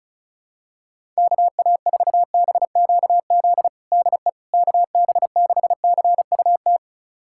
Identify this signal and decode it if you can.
Morse code, heard as a keyed tone, about 36 wpm, 700 Hz. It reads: KA4BQZ DE KB6CUT
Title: USA Callsign DE Callsign - 6 Characters - 35wpm